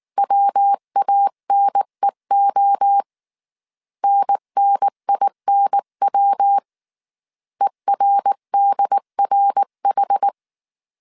Morse code, heard as a keyed tone, about 19 wpm, 800 Hz. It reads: WANEO NNINW ERDRH